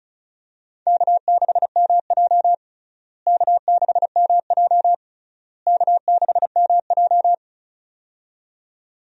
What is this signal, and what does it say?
Morse code, heard as a keyed tone, about 36 wpm, 700 Hz. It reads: K6MJ K6MJ K6MJ
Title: USA Call Signs - 4 Characters - Repeated 3x - 35wpm